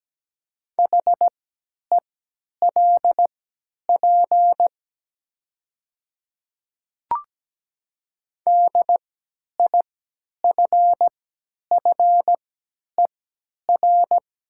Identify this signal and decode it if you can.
Morse code, heard as a keyed tone, about 17 wpm, spaced as for 10 wpm, 690 Hz. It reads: HELP DIFFER
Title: 3x Character Spacing - Top 200 Words - 17wpm